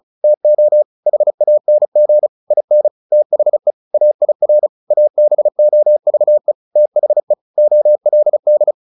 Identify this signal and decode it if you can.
Morse code, heard as a keyed tone, about 34 wpm, 600 Hz. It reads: TO HANG IN THE AIR ABOVE THE OLD
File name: the old man and the sea 2 35wpm